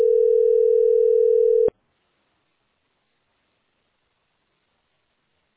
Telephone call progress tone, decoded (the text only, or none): ringback tone